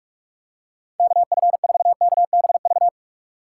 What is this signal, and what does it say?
Morse code, heard as a keyed tone, about 46 wpm, 700 Hz. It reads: KF4KBV
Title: USA Call Signs - 6 Characters - 3x Character Spacing - 45wpm